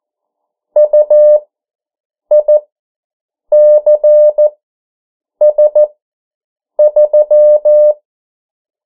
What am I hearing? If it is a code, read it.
Morse code, heard as a keyed tone, about 14 wpm, spaced as for 7 wpm, 600 Hz. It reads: UICS3
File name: morse mp3 / 09 urok 8